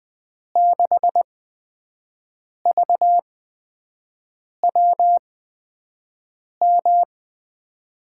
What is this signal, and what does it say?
Morse code, heard as a keyed tone, about 21 wpm, spaced as for 5 wpm, 700 Hz. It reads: 6VWM
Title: USA Call Signs - 6 Characters - 8x Character Spacing - 20wpm